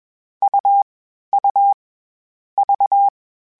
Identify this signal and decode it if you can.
Morse code, heard as a keyed tone, about 21 wpm, 800 Hz. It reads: U U V